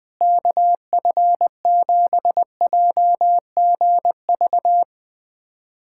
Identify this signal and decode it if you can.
Morse code, heard as a keyed tone, about 20 wpm, 700 Hz. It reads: KF7JGV
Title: USA Callsign DE Callsign 20wpm